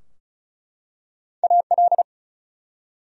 Morse code, heard as a keyed tone, about 36 wpm, 700 Hz. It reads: AL
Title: USA Common Names - Max 4 Letters - 35wpm